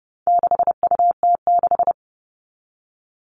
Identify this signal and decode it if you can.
Morse code, heard as a keyed tone, about 30 wpm, 700 Hz. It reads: 6UT6
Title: All Cut Numbers - 4-Digit 30wpm